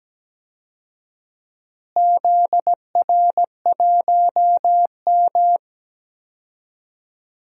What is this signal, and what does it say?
Morse code, heard as a keyed tone, about 17 wpm, 700 Hz. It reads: ZR1M
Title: DXCC-100 Call Signs - 4 Characters - Repeated 3x - 17wpm